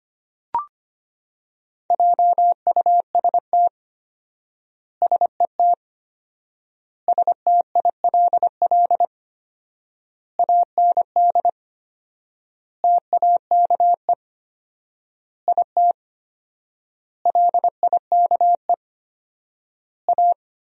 Morse code, heard as a keyed tone, about 25 wpm, 700 Hz. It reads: JUST SET STILL AND TAKE IT LIKE A
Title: 4x Word-Spacing - Sentences from Top 300 Words - 25wpm